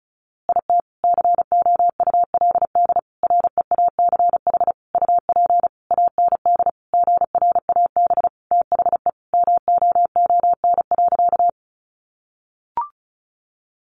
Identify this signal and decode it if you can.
Morse code, heard as a keyed tone, about 34 wpm, 700 Hz. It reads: IT COULD REACH UP AND GRAB THE MOON.
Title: A Sound Of Thunder 35wpm